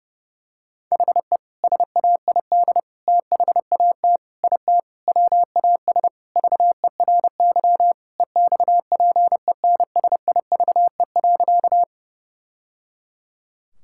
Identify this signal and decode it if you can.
Morse code, heard as a keyed tone, about 32 wpm, 700 Hz. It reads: HE SAID THAT IT WAS VERY EXPENSIVE.
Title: Sentences from Top 1000 Words - Review Part 6 - 30wpm